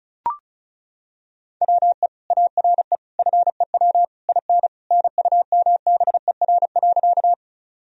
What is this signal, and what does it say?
Morse code, heard as a keyed tone, about 36 wpm, 700 Hz. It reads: WE ARE FEW IN NUMBER.
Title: Sentences from Top 1000 Words - Review Part 1 - 35wpm